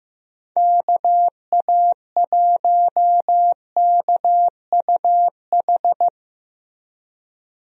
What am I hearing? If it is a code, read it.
Morse code, heard as a keyed tone, about 15 wpm, 700 Hz. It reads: KA1KUH
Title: USA Call Signs - 6 Characters - Repeated 3x - 15wpm